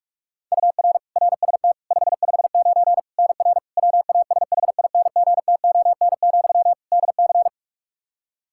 Morse code, heard as a keyed tone, about 46 wpm, 700 Hz. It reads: UR RST 559 NR WASHINGTON, DC
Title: K1USN SST QSO Practice 45wpm